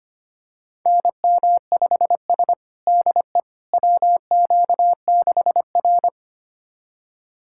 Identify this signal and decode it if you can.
Morse code, heard as a keyed tone, about 25 wpm, 700 Hz. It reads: NM5S DE WQ6R
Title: USA Callsign DE Callsign - 4 Characters - 25wpm